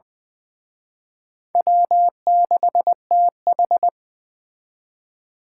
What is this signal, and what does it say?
Morse code, heard as a keyed tone, about 24 wpm, 700 Hz. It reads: W6TH
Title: Speed Racing - USA 1x2 Call Signs 20wpm